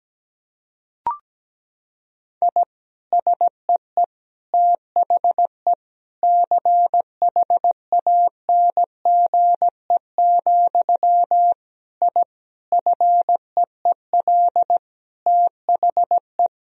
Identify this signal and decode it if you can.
Morse code, heard as a keyed tone, about 17 wpm, 700 Hz. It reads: I SEE THE CHANGE, I FEEL THE
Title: Sentences from Top 1000 Words - Review Part 1 - 17wpm